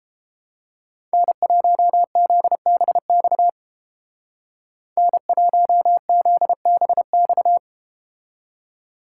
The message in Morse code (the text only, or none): N1ZBX N1ZBX